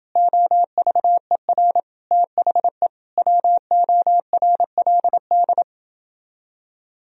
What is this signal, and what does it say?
Morse code, heard as a keyed tone, about 29 wpm, 700 Hz. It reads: OVER THE WORLD